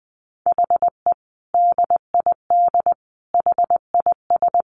Morse code, heard as a keyed tone, about 19 wpm, 700 Hz. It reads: HE DID HIS
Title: Sentences from Top 1500 Words 20wpm